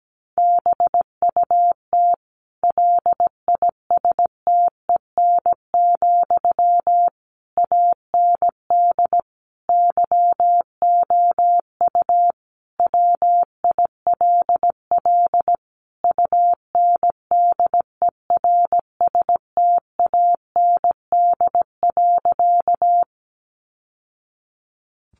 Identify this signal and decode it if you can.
Morse code, heard as a keyed tone, about 17 wpm, 700 Hz. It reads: BUT LISTEN, AND YOU WILL UNDERSTAND.
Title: Sentences from Top 600 Words 17wpm